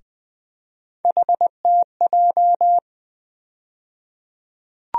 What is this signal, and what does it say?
Morse code, heard as a keyed tone, about 20 wpm, 700 Hz. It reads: HTJ
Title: USA Call Sign Postfixes - 3 Letters 20wpm